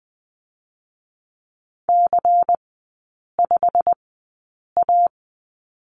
Morse code, heard as a keyed tone, about 20 wpm, 700 Hz. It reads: C 5 A
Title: 114 Lesson 19e - Letter w - Three Characters 20wpm